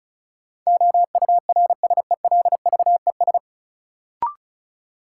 Morse code, heard as a keyed tone, about 34 wpm, 700 Hz. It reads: OURSELVES
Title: Sentences from Top 1600 Words - New Word Drill - 35wpm